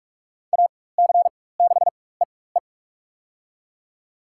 Morse code, heard as a keyed tone, about 41 wpm, spaced as for 20 wpm, 700 Hz. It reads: AC6EE
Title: USA Call Signs - 5 Characters - 4x Character Spacing - 45wpm